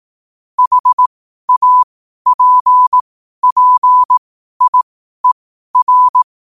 Morse code, heard as a keyed tone, about 18 wpm, spaced as for 12 wpm, 1000 Hz. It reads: HAPPIER
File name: qotdfiles / 12wpm